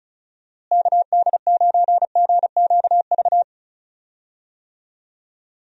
Morse code, heard as a keyed tone, about 34 wpm, 700 Hz. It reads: KD9GQV
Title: USA Call Signs - 6 Characters - Repeated 3x - 35wpm